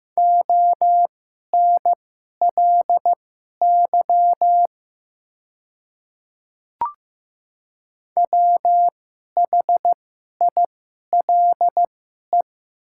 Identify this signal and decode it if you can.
Morse code, heard as a keyed tone, about 15 wpm, spaced as for 11 wpm, 700 Hz. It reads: ONLY WHILE
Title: 2x Character Spacing - Top 300 Words - 15wpm